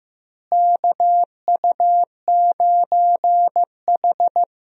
Morse code, heard as a keyed tone, about 16 wpm, 700 Hz. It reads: KU9H
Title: USA Call Signs - 4 Characters - Repeated 2x - 15wpm